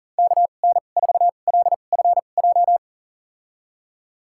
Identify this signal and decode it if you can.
Morse code, heard as a keyed tone, about 40 wpm, spaced as for 29 wpm, 700 Hz. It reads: KN4LFJ
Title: USA Call Signs - 6 Characters - 2x Character Spacing - 40wpm